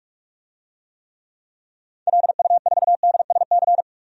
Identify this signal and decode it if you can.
Morse code, heard as a keyed tone, about 41 wpm, 700 Hz. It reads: LU4DSC
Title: DXCC-100 Call Signs - 6 Characters - 45wpm